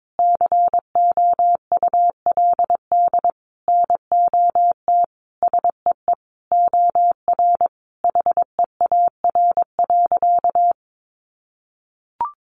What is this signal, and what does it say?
Morse code, heard as a keyed tone, about 22 wpm, 700 Hz. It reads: COULD NOT SEE OR HEAR.
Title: Sentences from Top 600 Words 22wpm